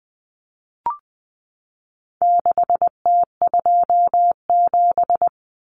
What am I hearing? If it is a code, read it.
Morse code, heard as a keyed tone, about 20 wpm, 700 Hz. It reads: 6T27
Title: Common Cut Numbers - 4-Digit 20wpm